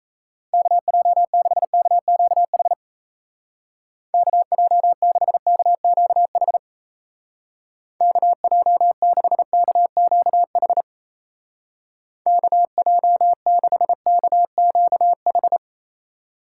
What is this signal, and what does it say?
Morse code, heard as a keyed tone, about 35 wpm, 700 Hz. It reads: KJ6KQH KJ6KQH KJ6KQH KJ6KQH
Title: Speed Racing - USA 2x3 Call Signs 28wpm